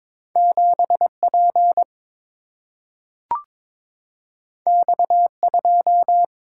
Morse code, heard as a keyed tone, about 22 wpm, 700 Hz. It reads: 7P X2
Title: Single-Letter Single-Number 22wpm